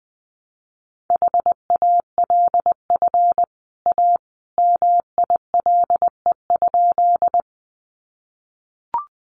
Morse code, heard as a keyed tone, about 20 wpm, 700 Hz. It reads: HALF A MILE?
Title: Sentences from Top 400 Words 20wpm